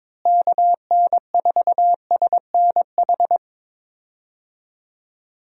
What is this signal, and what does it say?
Morse code, heard as a keyed tone, about 22 wpm, 700 Hz. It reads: KN4SNH
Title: USA Call Signs - 6 Characters - Repeated 2x - 22wpm